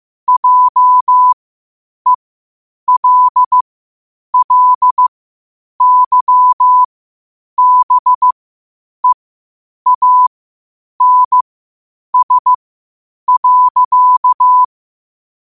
Morse code, heard as a keyed tone, about 15 wpm, spaced as for 8 wpm, 1000 Hz. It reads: JELLYBEANS.
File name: qotdfiles / 8wpm